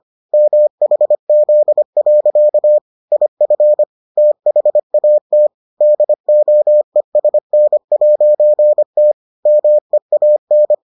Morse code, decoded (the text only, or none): MHZ. IF THAT DOESN'T MEAN